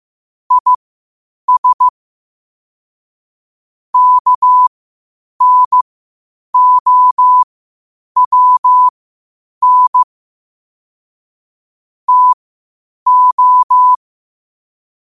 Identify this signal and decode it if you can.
Morse code, heard as a keyed tone, about 15 wpm, spaced as for 8 wpm, 1000 Hz. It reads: IS KNOWN TO
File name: qotdfiles / 8wpm